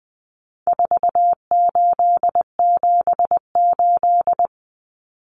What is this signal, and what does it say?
Morse code, heard as a keyed tone, about 20 wpm, 700 Hz. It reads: 4878
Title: Common Cut Numbers - 4-Digit 20wpm